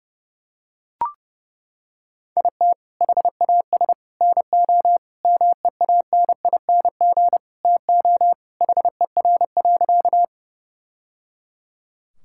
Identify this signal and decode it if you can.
Morse code, heard as a keyed tone, about 30 wpm, 700 Hz. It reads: IT HAS NO MEANING TO HER.
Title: Sentences from Top 1000 Words - Review Part 6 - 30wpm